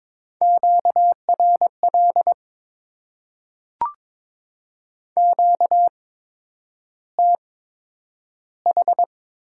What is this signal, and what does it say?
Morse code, heard as a keyed tone, about 22 wpm, spaced as for 9 wpm, 700 Hz. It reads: QRL QTH
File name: Top 20 QSO Elements - 8x Character Spacing - 22wpm